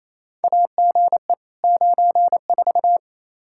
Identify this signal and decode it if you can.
Morse code, heard as a keyed tone, about 28 wpm, 700 Hz. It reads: AGE 94
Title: Practice QSO - Exchange 3 - More Info - 28wpm